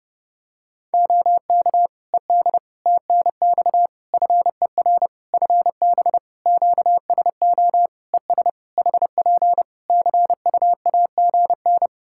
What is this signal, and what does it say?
Morse code, heard as a keyed tone, about 30 wpm, 700 Hz. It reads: OK ED TNX FER FB QSO ES HP CU AGN
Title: Practice QSO - Exchange 4 - Ending - 30wpm